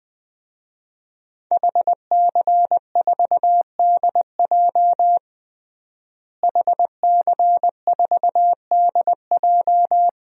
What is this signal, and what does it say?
Morse code, heard as a keyed tone, about 20 wpm, 700 Hz. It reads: HC4DJ HC4DJ
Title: DXCC-100 Call Signs - 5 Characters - Repeated 3x - 20wpm